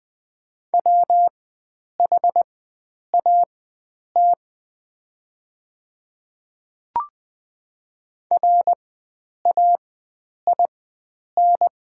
Morse code, heard as a keyed tone, about 20 wpm, spaced as for 9 wpm, 700 Hz. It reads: WHAT RAIN
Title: 4x Character Spacing - Top 300 Words - 20wpm